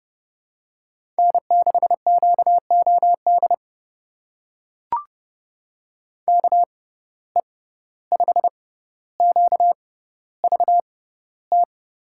Morse code, heard as a keyed tone, about 30 wpm, spaced as for 15 wpm, 700 Hz. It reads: N6QOD KE5QVT
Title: USA Call Signs - 6x Character Spacing - 30wpm